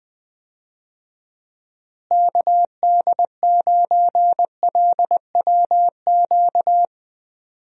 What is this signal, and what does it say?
Morse code, heard as a keyed tone, about 20 wpm, 700 Hz. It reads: KD9LWQ
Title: USA Call Signs - 6 Characters - Repeated 3x - 20wpm